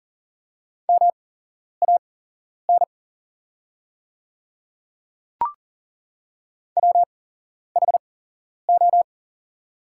Morse code, heard as a keyed tone, about 38 wpm, spaced as for 11 wpm, 700 Hz. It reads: MAN WHO
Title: 8x Character Spacing - Top 200 Words - 40wpm